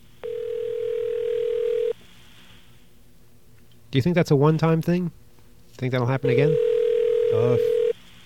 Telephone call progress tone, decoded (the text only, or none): ringback tone